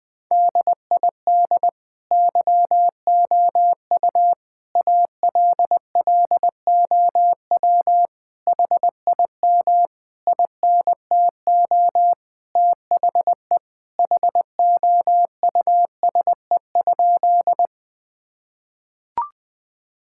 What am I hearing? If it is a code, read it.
Morse code, heard as a keyed tone, about 20 wpm, 700 Hz. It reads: DID YOU ALLOW HIM INTO THE HOUSE?
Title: Sentences from Top 1000 Words - Review Part 5 - 20wpm